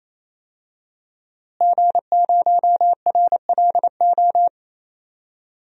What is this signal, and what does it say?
Morse code, heard as a keyed tone, about 28 wpm, 700 Hz. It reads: G0RLO
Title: DXCC-100 Call Signs - 5 Characters - Repeated 3x - 28wpm